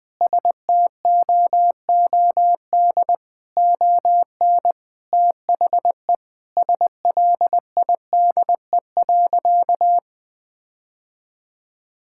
Morse code, heard as a keyed tone, about 21 wpm, 700 Hz. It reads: STOOD ON THE SLIDE.
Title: Sentences from Top 1600 Words 20wpm